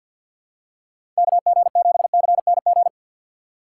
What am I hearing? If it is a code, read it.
Morse code, heard as a keyed tone, about 53 wpm, 700 Hz. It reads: KG7CNG
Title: USA Call Signs - 6 Characters - 4x Character Spacing - 50wpm